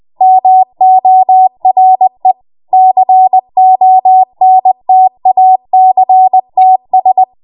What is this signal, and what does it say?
Morse code, heard as a keyed tone, about 20 wpm, 750 Hz. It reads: MORE CONTACTS
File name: Morse / 20WPM